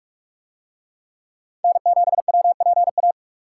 Morse code, heard as a keyed tone, about 46 wpm, 700 Hz. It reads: N7WPA